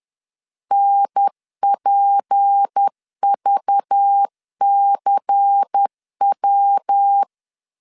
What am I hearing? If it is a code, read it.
Morse code, heard as a keyed tone, about 10 wpm, 800 Hz. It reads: NPVCW
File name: LEZIONI / 6 PKX0 / 6 P 10wpm 2432